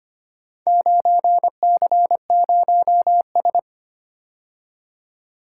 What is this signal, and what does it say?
Morse code, heard as a keyed tone, about 25 wpm, 700 Hz. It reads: 9C0S
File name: practice / 25wpm / Four Letter-Number Combinations - 25wpm